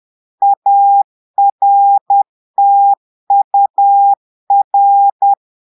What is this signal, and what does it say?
Morse code, heard as a keyed tone, about 10 wpm, 800 Hz. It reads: ARTUR
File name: Portugais - texte en clair 1 - vitesse - 10 mots minute